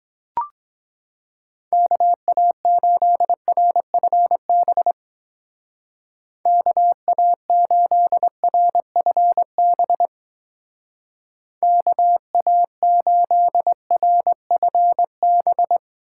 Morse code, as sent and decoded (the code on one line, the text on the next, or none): -.- .- ---.. .-. ..-. -... / -.- .- ---.. .-. ..-. -... / -.- .- ---.. .-. ..-. -...
KA8RFB KA8RFB KA8RFB